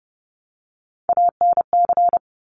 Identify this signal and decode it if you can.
Morse code, heard as a keyed tone, about 30 wpm, 700 Hz. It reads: ANC
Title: Top Three Letter Combinations in Words 30wpm